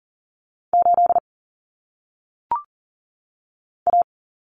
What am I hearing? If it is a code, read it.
Morse code, heard as a keyed tone, about 40 wpm, 700 Hz. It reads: 8 A
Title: Common Cut Numbers - 1-Digit 40wpm